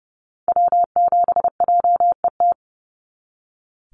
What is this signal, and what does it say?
Morse code, heard as a keyed tone, about 30 wpm, 700 Hz. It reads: W7JET
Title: NA SOTA Activator Callsigns 30wpm